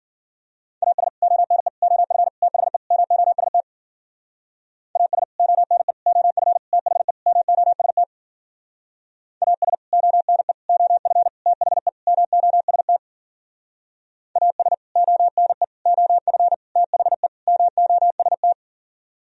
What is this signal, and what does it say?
Morse code, heard as a keyed tone, about 52 wpm, 700 Hz. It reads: AS ONE OF THE MOST AS ONE OF THE MOST AS ONE OF THE MOST AS ONE OF THE MOST